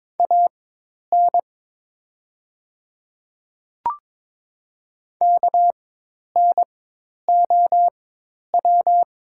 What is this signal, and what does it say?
Morse code, heard as a keyed tone, about 22 wpm, spaced as for 10 wpm, 690 Hz. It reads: AN KNOW
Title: 4x Character Spacing - Top 100 Words - 22wpm